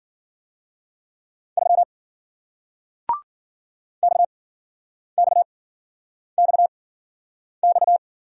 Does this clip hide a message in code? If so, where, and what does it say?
Yes: Morse code from 1.6 to 8.0 s, heard as a keyed tone, about 53 wpm, spaced as for 9 wpm, 700 Hz. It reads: 3 XXXX